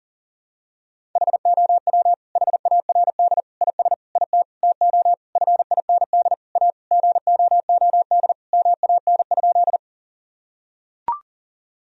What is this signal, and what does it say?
Morse code, heard as a keyed tone, about 37 wpm, 700 Hz. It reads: HOW HARD IS IT TO FIND A GOOD MAN?
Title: Sentences from Top 1000 Words - Review Part 1 - 40wpm